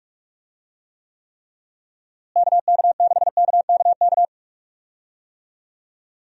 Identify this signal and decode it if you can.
Morse code, heard as a keyed tone, about 41 wpm, 700 Hz. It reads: KK6KKK
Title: USA Call Signs - 6 Characters - Repeated 3x - 45wpm